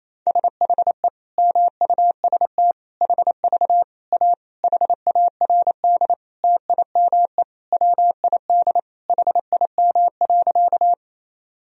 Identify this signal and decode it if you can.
Morse code, heard as a keyed tone, about 28 wpm, 700 Hz. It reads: SHE MUST HV A HARD TIME WID HIM.
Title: Sentences from Top 300 Words and Abbreviations 28wpm